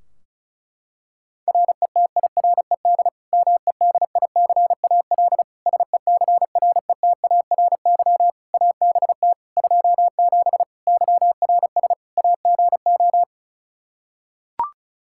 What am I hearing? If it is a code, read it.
Morse code, heard as a keyed tone, about 36 wpm, 700 Hz. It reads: RETIRED MEDICAL SECRETARY ABT 27 YRS AGO
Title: Practice QSO - Exchange 3 - More Info - 35wpm